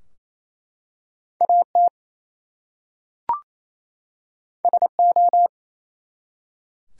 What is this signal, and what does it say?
Morse code, heard as a keyed tone, about 29 wpm, 700 Hz. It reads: AT SO
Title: Two Letter Words - 28wpm